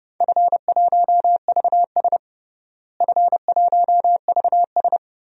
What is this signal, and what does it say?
Morse code, heard as a keyed tone, about 32 wpm, 700 Hz. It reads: F1VS F1VS
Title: DXCC-100 Call Signs - 4 Characters - Repeated 2x - 30wpm